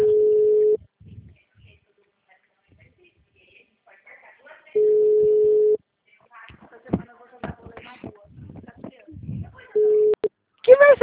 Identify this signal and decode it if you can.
Ringback tone (Europe, 425 Hz), with speech.